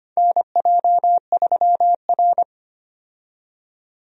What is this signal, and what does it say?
Morse code, heard as a keyed tone, about 25 wpm, 700 Hz. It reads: NJ3R
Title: USA Call Signs - 4 Characters - 25wpm